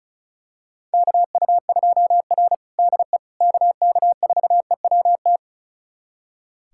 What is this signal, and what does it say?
Morse code, heard as a keyed tone, about 36 wpm, 700 Hz. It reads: KU2R DE KK4EWT
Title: USA Callsign DE Callsign 35wpm